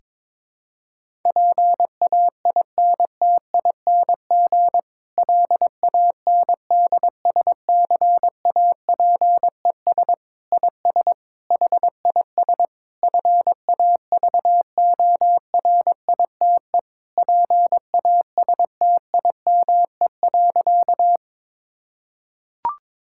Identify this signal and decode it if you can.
Morse code, heard as a keyed tone, about 22 wpm, 700 Hz. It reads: PAINTING LANDSCAPES IS HIS FAVORITE PASTIME.
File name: Difficult Sentences with Hints - 22wpm